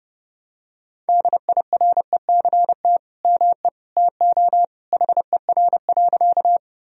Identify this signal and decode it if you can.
Morse code, heard as a keyed tone, about 30 wpm, 700 Hz. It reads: DIRECT ME TO HER.
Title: Sentences from Top 1000 Words - Review Part 2 - 30wpm